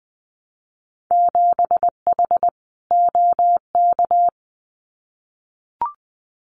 Morse code, heard as a keyed tone, about 20 wpm, 700 Hz. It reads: 7H OK
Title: Winter Field Day Exchanges Encore 20wpm